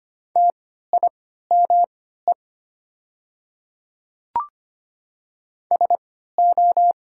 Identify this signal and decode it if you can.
Morse code, heard as a keyed tone, about 25 wpm, spaced as for 14 wpm, 700 Hz. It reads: TIME SO
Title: 3x Character Spacing - Top 100 Words - 25wpm